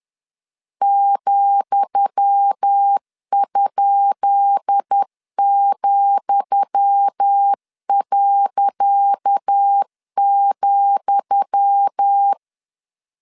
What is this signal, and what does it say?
Morse code, heard as a keyed tone, about 10 wpm, 800 Hz. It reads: ,?,.,
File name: LEZIONI / 10 SIGNS / 10D COMA 10wpm 3013